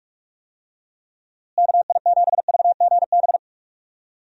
Morse code, heard as a keyed tone, about 44 wpm, 700 Hz. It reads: KI7VGB
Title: USA Call Signs - 6 Characters - Repeated 2x - 45wpm